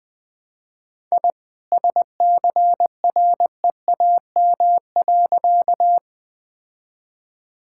Morse code, heard as a keyed tone, about 21 wpm, 700 Hz. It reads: I SCREAM.
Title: All-Age Appropriate Jokes - 20wpm